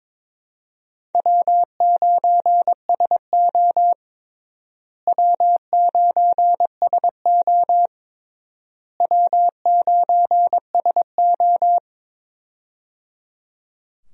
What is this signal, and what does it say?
Morse code, heard as a keyed tone, about 22 wpm, 700 Hz. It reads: W9SO W9SO W9SO
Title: USA Call Signs - 4 Characters - Repeated 3x - 22wpm